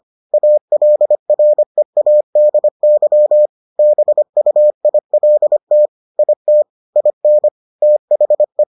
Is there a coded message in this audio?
Yes: Morse code from 0.3 to 8.6 s, heard as a keyed tone, about 25 wpm, 600 Hz. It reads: ALREADY BUILT IT IN THE